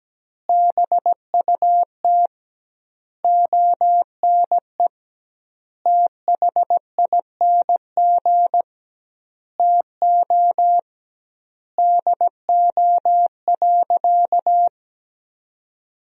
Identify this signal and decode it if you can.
Morse code, heard as a keyed tone, about 17 wpm, 700 Hz. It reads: BUT ONE THING TO DO.
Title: 2x Word-Spacing - Sentences from Top 100 Words - 17wpm